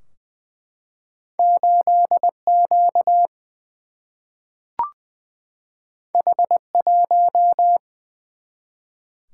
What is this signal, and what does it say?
Morse code, heard as a keyed tone, about 20 wpm, 700 Hz. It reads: 8Q H1
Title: Single-Letter Single-Number 20wpm